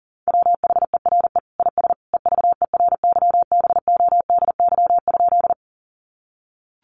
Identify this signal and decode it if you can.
Morse code, heard as a keyed tone, about 40 wpm, 700 Hz. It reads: WHERE IS EVERYBODY?
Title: Sentences from Top 1400 Words 40wpm